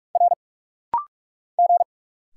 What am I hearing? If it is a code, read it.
Morse code, heard as a keyed tone, about 44 wpm, 700 Hz. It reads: R G
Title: Single Letter-Number - Rapid-Fire 45wpm